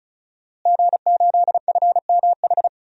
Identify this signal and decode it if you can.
Morse code, heard as a keyed tone, about 34 wpm, 700 Hz. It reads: G8FMH
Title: DXCC-100 Call Signs - 5 Characters - Repeated 3x - 35wpm